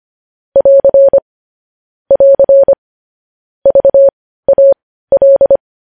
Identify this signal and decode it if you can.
Morse code, heard as a keyed tone, about 25 wpm, spaced as for 15 wpm, 550 Hz. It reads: <AR> <AR> VAL
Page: news headlines in morse code at 15 wpm